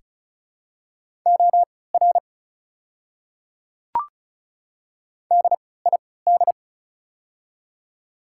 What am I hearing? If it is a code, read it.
Morse code, heard as a keyed tone, about 36 wpm, spaced as for 20 wpm, 700 Hz. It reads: OR DID
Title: 3x Character Spacing - Top 100 Words - 35wpm